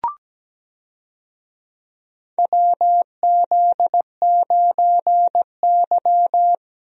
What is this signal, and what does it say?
Morse code, heard as a keyed tone, about 17 wpm, 700 Hz. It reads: WZ9Y
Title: USA Call Signs - 4 Characters - 17wpm